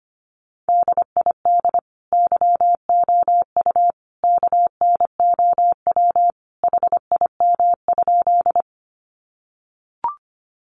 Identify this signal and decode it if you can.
Morse code, heard as a keyed tone, about 25 wpm, 700 Hz. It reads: DID YOU KNOW HIM?
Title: Sentences from Top 100 Words 25wpm